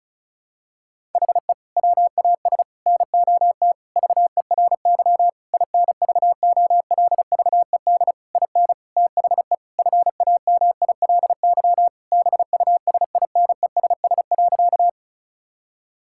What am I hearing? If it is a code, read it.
Morse code, heard as a keyed tone, about 36 wpm, 700 Hz. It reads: HE WAS NOT VERY INVOLVED IN THE FAMILY BUSINESS.
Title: Sentences from Top 1000 Words - Review Part 5 - 35wpm